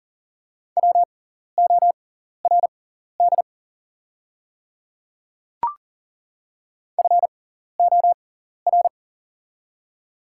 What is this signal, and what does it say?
Morse code, heard as a keyed tone, about 40 wpm, spaced as for 14 wpm, 700 Hz. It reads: WORD FOR
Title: 6x Character Spacing - Top 200 Words - 40wpm